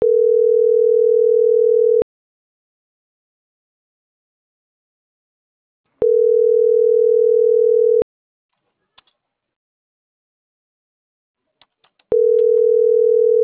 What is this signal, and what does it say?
Ringback tone (North America), 440 + 480 Hz.